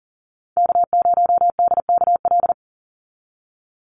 Morse code, heard as a keyed tone, about 40 wpm, 690 Hz. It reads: K0DKL
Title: USA Call Signs 40wpm